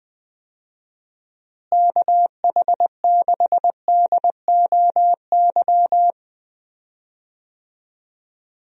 Morse code, heard as a keyed tone, about 20 wpm, 700 Hz. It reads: KH6DOY
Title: DXCC-100 Call Signs - 6 Characters - Repeated 2x - 20wpm